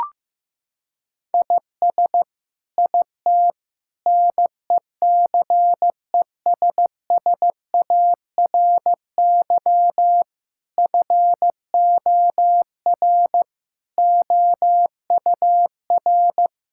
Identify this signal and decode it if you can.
Morse code, heard as a keyed tone, about 15 wpm, 700 Hz. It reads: IS IT NECESSARY FOR OUR
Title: Sentences from Top 1000 Words - Review Part 6 - 15wpm